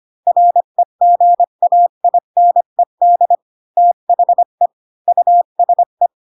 Morse code, heard as a keyed tone, about 25 wpm, 700 Hz. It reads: REGAINED THE USE
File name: Pinocchio / Pinocchio ch1 20 25 ik1hge